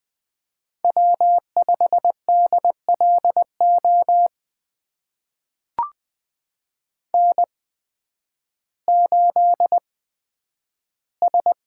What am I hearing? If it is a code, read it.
Morse code, heard as a keyed tone, about 20 wpm, spaced as for 11 wpm, 700 Hz. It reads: W5DLO N8S
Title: USA Call Signs - 5 Characters - 8x Character Spacing - 20wpm